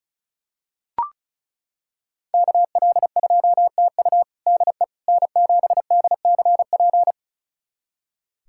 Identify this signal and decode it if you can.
Morse code, heard as a keyed tone, about 36 wpm, 700 Hz. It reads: KL2TU DE N7DCP
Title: USA Callsign DE Callsign - 5 Characters - 35wpm